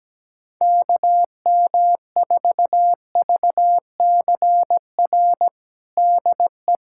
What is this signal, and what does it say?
Morse code, heard as a keyed tone, about 17 wpm, 700 Hz. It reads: KM4VCR DE
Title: USA Callsign DE Callsign 17wpm